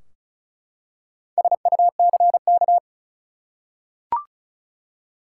Morse code, heard as a keyed tone, about 36 wpm, 700 Hz. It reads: SUCK
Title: Four Letter Words - 35wpm